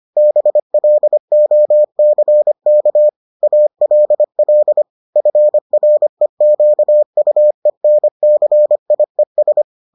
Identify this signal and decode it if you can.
Morse code, heard as a keyed tone, about 25 wpm, 600 Hz. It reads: BLOCK ALL FREQUENCIES